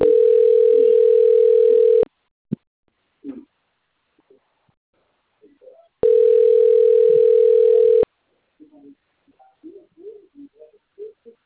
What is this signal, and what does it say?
Ringback tone (North America), 440 + 480 Hz.
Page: Duque de Caxias